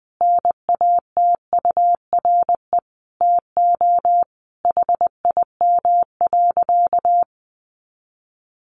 Morse code, heard as a keyed tone, about 19 wpm, 700 Hz. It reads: NATURE TO HIM.
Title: Sentences from Top 600 Words 20wpm